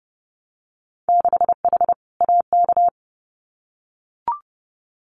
Morse code, heard as a keyed tone, about 30 wpm, 700 Hz. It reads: 6H AK
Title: Winter Field Day Exchanges Encore 30wpm